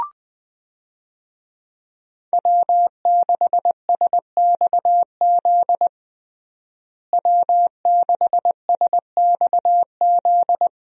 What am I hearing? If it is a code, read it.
Morse code, heard as a keyed tone, about 21 wpm, 700 Hz. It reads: W6SXZ W6SXZ
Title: USA Call Signs - 5 Characters - Repeated 3x - 20wpm